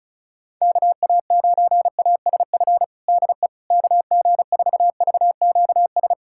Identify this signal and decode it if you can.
Morse code, heard as a keyed tone, about 36 wpm, 700 Hz. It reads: KA9ASF DE KG4VQS
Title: USA Callsign DE Callsign 35wpm